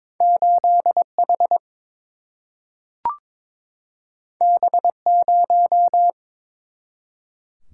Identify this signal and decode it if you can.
Morse code, heard as a keyed tone, about 22 wpm, 700 Hz. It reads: 8 H B 0
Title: Single-Letter Single-Number 22wpm